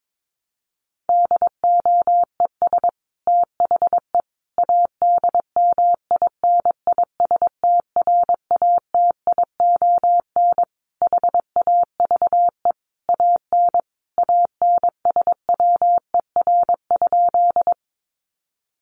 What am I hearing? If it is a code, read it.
Morse code, heard as a keyed tone, about 22 wpm, 700 Hz. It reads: DOES THE ADMINISTRATION HAVE AN ANSWER?
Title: Sentences from Top 1300 Words 22wpm